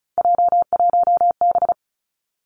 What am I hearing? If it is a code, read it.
Morse code, heard as a keyed tone, about 36 wpm, 700 Hz. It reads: J1B
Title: Three Challenging Letter-Number Combinations 35wpm